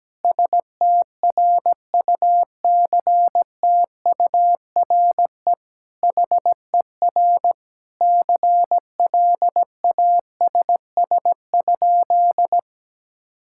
Morse code, heard as a keyed tone, about 17 wpm, 700 Hz. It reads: STRUCTURE HER CLASS?
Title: Sentences from Top 1000 Words - Review Part 6 - 17wpm